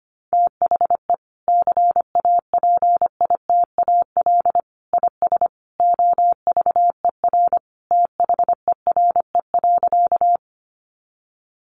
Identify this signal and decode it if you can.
Morse code, heard as a keyed tone, about 25 wpm, 700 Hz. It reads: THE CAPITAL IS OVER THERE.
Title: Sentences from Top 1000 Words 25wpm